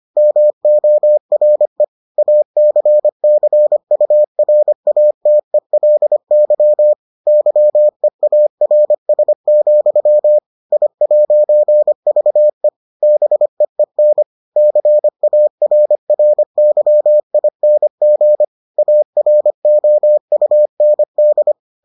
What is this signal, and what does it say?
Morse code, heard as a keyed tone, about 25 wpm, 600 Hz. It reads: MORE ACCURATELY YEARS, I'VE BEEN CARRYING AROUND